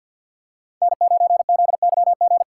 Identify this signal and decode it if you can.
Morse code, heard as a keyed tone, about 50 wpm, 700 Hz. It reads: N9ZYG
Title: USA Call Signs - 5 Characters - 6x Character Spacing - 50wpm